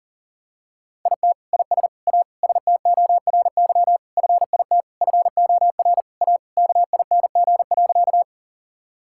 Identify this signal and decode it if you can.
Morse code, heard as a keyed tone, about 40 wpm, 700 Hz. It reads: IT IS A STORY FIT FOR A KING.
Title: Sentences from Top 1000 Words - Review Part 5 - 40wpm